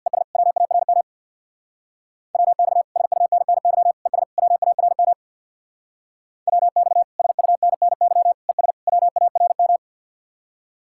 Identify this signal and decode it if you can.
Morse code, heard as a keyed tone, about 55 wpm, 700 Hz. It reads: ES WARM WX SUNNY ES WARM WX SUNNY ES WARM
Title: Speed-Racing Common QSO Phrases - 50wpm